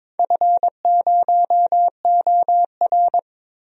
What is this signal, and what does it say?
Morse code, heard as a keyed tone, about 21 wpm, 700 Hz. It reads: F0OR
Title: DXCC-100 Call Signs - 4 Characters - Repeated 2x - 22wpm